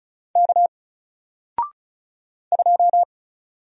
Morse code, heard as a keyed tone, about 36 wpm, 700 Hz. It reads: K 2
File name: Single-Letter-Number-35wpm